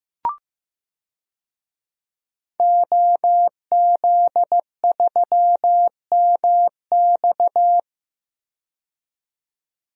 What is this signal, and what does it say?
Morse code, heard as a keyed tone, about 15 wpm, 700 Hz. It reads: OZ3MX
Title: DXCC-100 Call Signs - 5 Characters - 15wpm